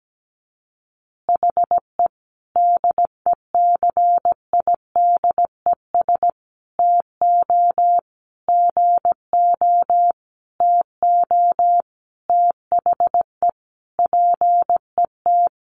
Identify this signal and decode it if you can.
Morse code, heard as a keyed tone, about 17 wpm, 700 Hz. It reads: HE DECIDES TO GO TO THE PET
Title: Additional Easy Phrases 17wpm